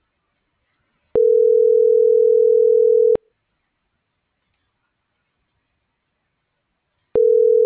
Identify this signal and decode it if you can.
Ringback tone (North America), 440 + 480 Hz.